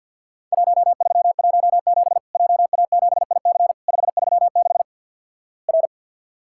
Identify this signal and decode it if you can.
Morse code, heard as a keyed tone, about 50 wpm, 700 Hz. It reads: 1317 JA7IC 536 R
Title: Worked All Europe DX Contest - QTC Exchange with Repeats - 50wpm